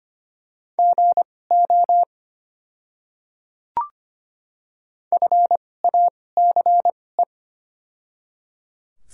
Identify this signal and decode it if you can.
Morse code, heard as a keyed tone, about 25 wpm, spaced as for 18 wpm, 700 Hz. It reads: GO FACE
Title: 2x Character Spacing - Top 300 Words - 25wpm